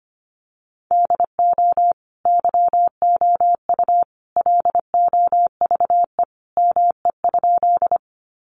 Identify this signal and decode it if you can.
Morse code, heard as a keyed tone, about 25 wpm, 700 Hz. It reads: DO YOU LOVE ME?
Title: Sentences from Top 400 Words 25wpm